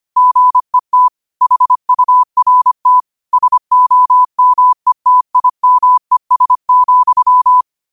Morse code, heard as a keyed tone, about 25 wpm, 1000 Hz. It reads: GET HURT SOMETIMES,